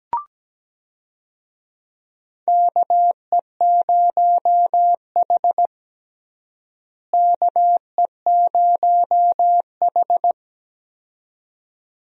Morse code, heard as a keyed tone, about 17 wpm, 700 Hz. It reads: KE0H KE0H
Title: USA Call Signs - 4 Characters - Repeated 2x - 17wpm